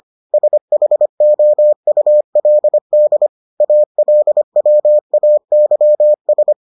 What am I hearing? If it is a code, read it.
Morse code, heard as a keyed tone, about 25 wpm, 600 Hz. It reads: SHOULD ALWAYS